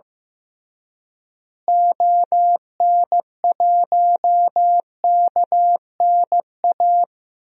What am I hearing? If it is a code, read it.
Morse code, heard as a keyed tone, about 16 wpm, 700 Hz. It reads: ON1KNA
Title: DXCC-100 Call Signs - 6 Characters - Repeated 3x - 15wpm